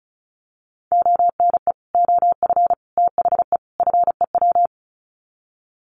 Morse code, heard as a keyed tone, about 34 wpm, 700 Hz. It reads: ONE OF THE FEW
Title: Sets of 4 Words 35wpm